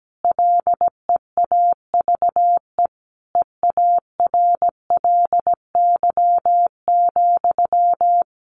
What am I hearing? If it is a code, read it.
Morse code, heard as a keyed tone, about 17 wpm, 700 Hz. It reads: LEAVE EARLY,
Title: Sentences from Top 400 Words 17wpm